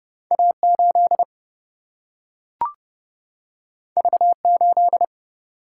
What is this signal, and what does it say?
Morse code, heard as a keyed tone, about 30 wpm, 700 Hz. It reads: A 8 V 8
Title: Single-Letter Single-Number 30wpm